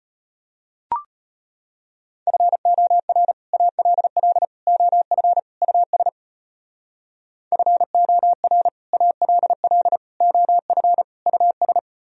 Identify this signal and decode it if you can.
Morse code, heard as a keyed tone, about 33 wpm, 700 Hz. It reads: FOR ALL OF US FOR ALL OF US